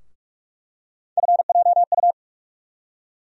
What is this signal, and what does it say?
Morse code, heard as a keyed tone, about 47 wpm, 700 Hz. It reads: FJU
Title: USA Call Sign Postfixes - 3 Letters 45wpm